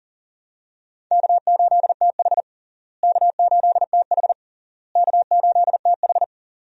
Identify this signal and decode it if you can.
Morse code, heard as a keyed tone, about 40 wpm, 700 Hz. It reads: K8TH K8TH K8TH
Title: USA Call Signs - 4 Characters - Repeated 3x - 40wpm